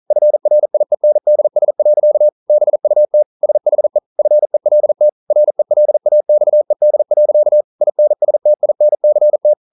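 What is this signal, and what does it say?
Morse code, heard as a keyed tone, about 42 wpm, 600 Hz. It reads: FRIENDS. BUT SHE FELT RELAXED. INSTINCT
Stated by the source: Morse code practice at 41 wpm, from CW med Gnister.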